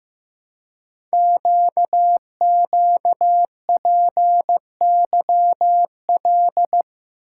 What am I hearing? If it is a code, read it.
Morse code, heard as a keyed tone, about 15 wpm, 700 Hz. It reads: QQPYL
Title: Code Groups - 5 Letters - 15wpm